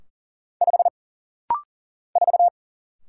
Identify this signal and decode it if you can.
Morse code, heard as a keyed tone, about 40 wpm, 700 Hz. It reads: 5 4
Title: Single Letter-Number - Rapid-Fire 40wpm